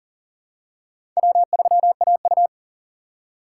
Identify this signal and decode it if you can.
Morse code, heard as a keyed tone, about 37 wpm, 700 Hz. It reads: W3AU
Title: USA Call Signs - 2x Character Spacing - 40wpm